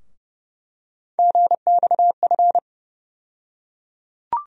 Morse code, heard as a keyed tone, about 30 wpm, 700 Hz. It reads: GXF
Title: USA Call Sign Postfixes - 3 Letters 30wpm